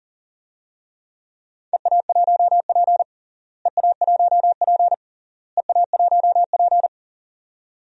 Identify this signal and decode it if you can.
Morse code, heard as a keyed tone, about 40 wpm, 700 Hz. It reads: EA1P EA1P EA1P
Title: DXCC-100 Call Signs - 4 Characters - Repeated 3x - 40wpm